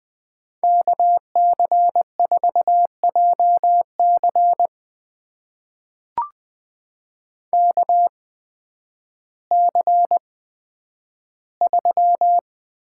Morse code, heard as a keyed tone, about 20 wpm, spaced as for 11 wpm, 700 Hz. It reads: KC4JC KC3
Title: USA Call Signs - 8x Character Spacing - 20wpm